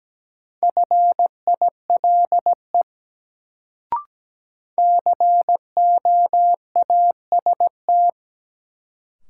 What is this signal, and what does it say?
Morse code, heard as a keyed tone, about 17 wpm, 700 Hz. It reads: FILE COAST
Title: Sentences from Top 1000 Words - Review Part 4 - New Word Drill - 17wpm